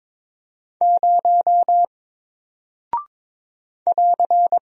Morse code, heard as a keyed tone, about 22 wpm, 700 Hz. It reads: 0 <AR>
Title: Single Character (Letter:Number:Punctuation:Prosign) Alt 22wpm